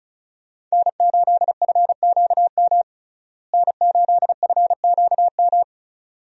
Morse code, heard as a keyed tone, about 34 wpm, 700 Hz. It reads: N8FQM N8FQM
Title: USA Call Signs - 5 Characters - Repeated 2x - 35wpm